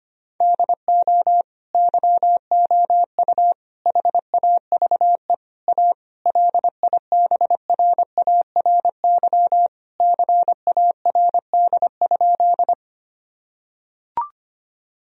Morse code, heard as a keyed tone, about 25 wpm, 700 Hz. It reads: DO YOU HAVE A LIBRARY CARD?
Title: Sentences from Top 1000 Words - Review Part 4 - 25wpm